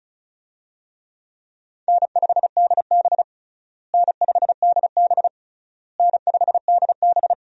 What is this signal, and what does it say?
Morse code, heard as a keyed tone, about 36 wpm, 700 Hz. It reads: N5DB N5DB N5DB
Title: USA Call Signs - 4 Characters - Repeated 3x - 35wpm